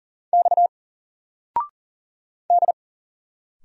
Morse code, heard as a keyed tone, about 40 wpm, 700 Hz. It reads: X D
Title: Single Letter-Number 40wpm